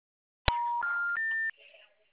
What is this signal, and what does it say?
Special information tone, with speech.
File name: out- -2016- - - .63814